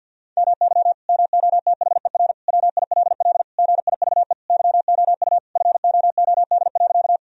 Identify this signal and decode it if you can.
Morse code, heard as a keyed tone, about 50 wpm, 700 Hz. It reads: MY MOTHER WILL GIVE YOU FOOD.
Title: Sentences from Top 1000 Words - Review Part 1 - 50wpm